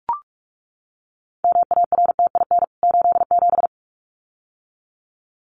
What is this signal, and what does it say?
Morse code, heard as a keyed tone, about 44 wpm, 700 Hz. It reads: MARTIN 87
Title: CWT Exchange 45wpm